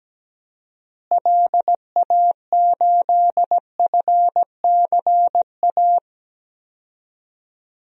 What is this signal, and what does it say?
Morse code, heard as a keyed tone, about 17 wpm, 700 Hz. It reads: LA8FCA
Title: DXCC-100 Call Signs - 6 Characters - Repeated 2x - 17wpm